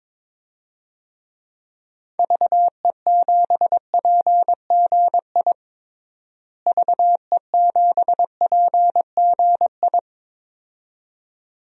Morse code, heard as a keyed tone, about 22 wpm, 700 Hz. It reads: VE7PGI VE7PGI
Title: DXCC-100 Call Signs - 6 Characters - Repeated 2x - 22wpm